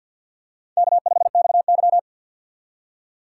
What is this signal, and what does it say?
Morse code, heard as a keyed tone, about 52 wpm, 700 Hz. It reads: K5XY
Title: USA Call Signs - 4 Characters - 3x Character Spacing - 50wpm